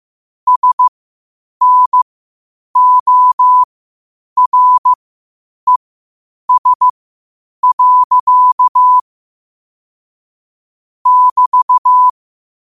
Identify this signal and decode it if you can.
Morse code, heard as a keyed tone, about 15 wpm, spaced as for 8 wpm, 1000 Hz. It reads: SNORES. <BT>